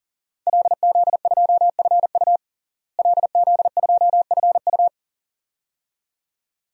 Morse code, heard as a keyed tone, about 40 wpm, 700 Hz. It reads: LZ2FU LZ2FU
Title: DXCC-100 Call Signs - 5 Characters - Repeated 2x - 40wpm